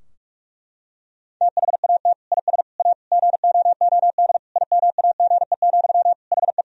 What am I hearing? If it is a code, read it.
Morse code, heard as a keyed tone, about 46 wpm, 700 Hz. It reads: THAT IS A GOOD IMAGE, HE
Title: Sentences from Top 1000 Words - Review Part 4 - 45wpm